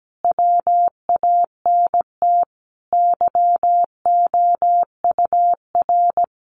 Morse code, heard as a keyed tone, about 17 wpm, 700 Hz. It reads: WANT YOUR
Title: Sentences from Top 1300 Words 17wpm